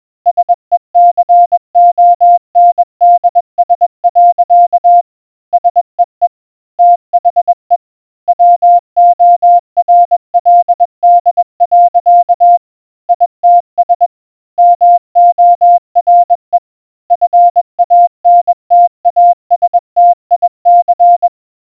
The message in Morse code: SECONDS. SEE THE WORLD. ITS MORE FANTASTIC